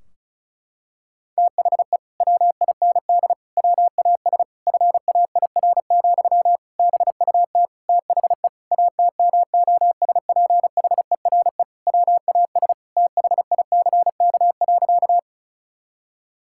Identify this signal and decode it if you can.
Morse code, heard as a keyed tone, about 36 wpm, 700 Hz. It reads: THE WIND WAS FAIR, BUT THE ATMOSPHERE WAS THICK.
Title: Sentences from Top 1600 Words 35wpm